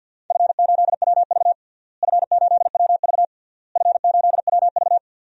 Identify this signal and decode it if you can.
Morse code, heard as a keyed tone, about 50 wpm, 700 Hz. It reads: F8WV F8WV F8WV
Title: DXCC-100 Call Signs - 4 Characters - Repeated 3x - 50wpm